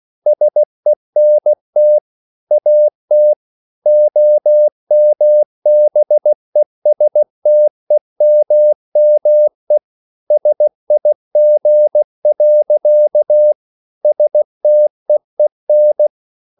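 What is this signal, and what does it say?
Morse code, heard as a keyed tone, about 16 wpm, 600 Hz. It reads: SENT AT OMBESTEMME SIG. STEEN
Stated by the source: Slow (QRS) Morse code at 16 wpm, from CW med Gnister.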